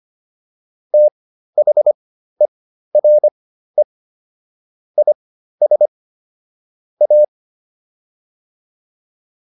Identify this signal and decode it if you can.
Morse code, heard as a keyed tone, about 25 wpm, spaced as for 13 wpm, 600 Hz. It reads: THERE IS A